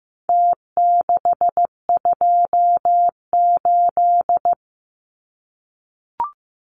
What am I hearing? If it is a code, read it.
Morse code, heard as a keyed tone, about 15 wpm, 700 Hz. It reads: T628
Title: Numbers - Contest Style - 4-Digit 15wpm